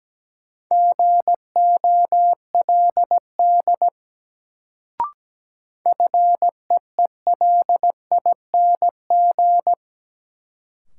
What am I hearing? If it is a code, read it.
Morse code, heard as a keyed tone, about 17 wpm, 700 Hz. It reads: GOLD FEELING
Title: Sentences from Top 1000 Words - Review Part 3 - New Word Drill - 17wpm